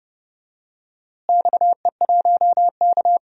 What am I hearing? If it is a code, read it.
Morse code, heard as a keyed tone, about 30 wpm, 700 Hz. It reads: XE1K
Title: DXCC-100 Call Signs - 4 Characters - Repeated 2x - 30wpm